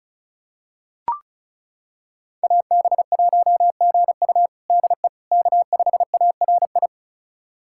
Morse code, heard as a keyed tone, about 36 wpm, 700 Hz. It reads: AB1GU DE K5ARI
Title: USA Callsign DE Callsign - 5 Characters - 35wpm